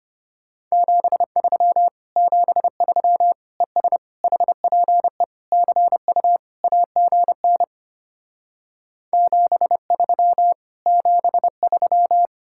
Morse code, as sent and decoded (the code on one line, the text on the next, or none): --... ...-- / --... ...-- / . ... / .... .--. / -.-. ..- / .- --. -. / --... ...-- / --... ...--
73 73 ES HPE CU AGN 73 73